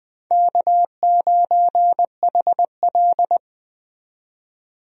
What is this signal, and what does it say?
Morse code, heard as a keyed tone, about 20 wpm, 700 Hz. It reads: K9HL
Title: USA Call Signs - 4 Characters - 20wpm